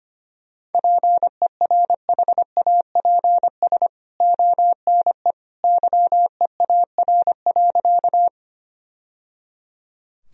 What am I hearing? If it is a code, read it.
Morse code, heard as a keyed tone, about 25 wpm, 700 Hz. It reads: PERHAPS ONE YEAR.
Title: Sentences from Top 1000 Words - Review Part 1 - 25wpm